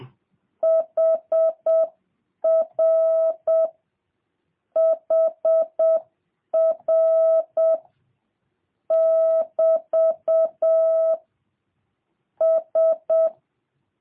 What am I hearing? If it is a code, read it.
Morse code, heard as a keyed tone, about 7 wpm, 650 Hz. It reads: HR HR <BT> S
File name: jh8cbh / 2023cw